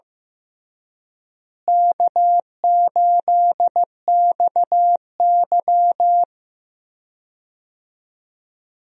Morse code, heard as a keyed tone, about 15 wpm, 700 Hz. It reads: K8XY
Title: USA Call Signs - 4 Characters - Repeated 3x - 15wpm